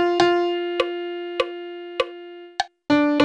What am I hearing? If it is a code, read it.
There is no code.